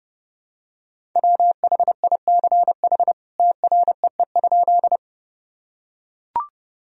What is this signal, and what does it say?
Morse code, heard as a keyed tone, about 30 wpm, 700 Hz. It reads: WHICH TREE?
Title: Sentences from Top 1000 Words - Review Part 1 - 30wpm